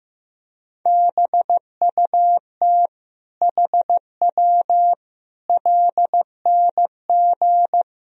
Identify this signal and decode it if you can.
Morse code, heard as a keyed tone, about 16 wpm, 700 Hz. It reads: BUT HW LNG
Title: Sentences from Top 500 Words and Abbreviations 15wpm